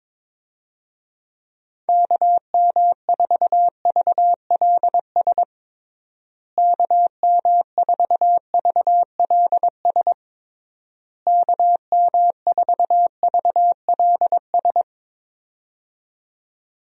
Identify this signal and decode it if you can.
Morse code, heard as a keyed tone, about 21 wpm, 700 Hz. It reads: KM4VLS KM4VLS KM4VLS
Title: USA Call Signs - 6 Characters - Repeated 3x - 22wpm